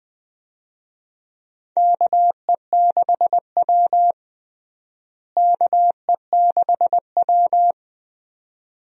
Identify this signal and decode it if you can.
Morse code, heard as a keyed tone, about 20 wpm, 700 Hz. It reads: KE6W KE6W
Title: USA Call Signs - 4 Characters - Repeated 2x - 20wpm